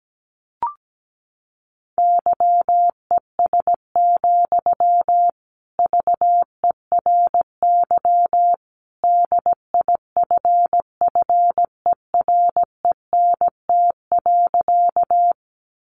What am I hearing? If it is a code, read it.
Morse code, heard as a keyed tone, about 17 wpm, 700 Hz. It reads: YES, VERY DIFFERENT.
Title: Sentences from Top 600 Words 17wpm